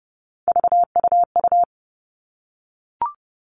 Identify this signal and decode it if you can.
Morse code, heard as a keyed tone, about 30 wpm, 700 Hz. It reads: VUU
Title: All Cut Numbers - 3-Digit 30wpm